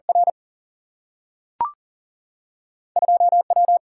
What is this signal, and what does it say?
Morse code, heard as a keyed tone, about 40 wpm, 700 Hz. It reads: R 2 W
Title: Single-Letter Single-Number 40wpm